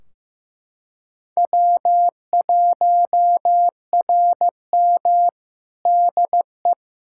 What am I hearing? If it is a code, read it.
Morse code, heard as a keyed tone, about 15 wpm, 700 Hz. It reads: W1RM DE
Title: K1USN SST QSO Practice 15wpm